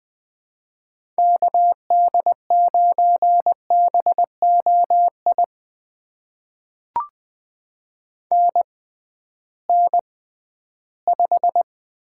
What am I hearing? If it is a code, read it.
Morse code, heard as a keyed tone, about 20 wpm, spaced as for 13 wpm, 700 Hz. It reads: KD9BOI NN5
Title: USA Call Signs - 6x Character Spacing - 20wpm